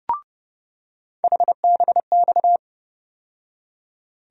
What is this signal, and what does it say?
Morse code, heard as a keyed tone, about 30 wpm, 700 Hz. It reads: HBX
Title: USA Call Sign Postfixes - 3 Letters 30wpm